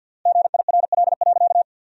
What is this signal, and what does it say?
Morse code, heard as a keyed tone, about 50 wpm, 700 Hz. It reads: GIRL.
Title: 3x Word-Spacing - Sentences from Top 400 Words - 50wpm